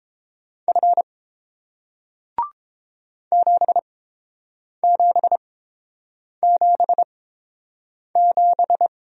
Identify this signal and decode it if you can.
Morse code, heard as a keyed tone, about 31 wpm, spaced as for 7 wpm, 700 Hz. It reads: F 7777